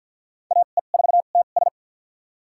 Morse code, heard as a keyed tone, about 52 wpm, spaced as for 36 wpm, 700 Hz. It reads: AE4TS
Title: USA Call Signs - 5 Characters - 2x Character Spacing - 50wpm